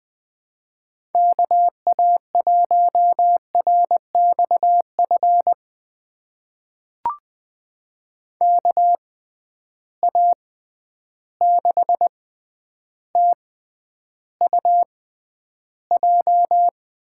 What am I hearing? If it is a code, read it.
Morse code, heard as a keyed tone, about 21 wpm, spaced as for 10 wpm, 700 Hz. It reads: KA1RXF KA6TUJ
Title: USA Call Signs - 6 Characters - 6x Character Spacing - 20wpm